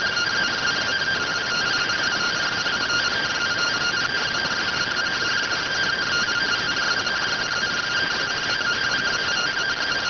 Two-tone teleprinter signal, about 170 Hz shift.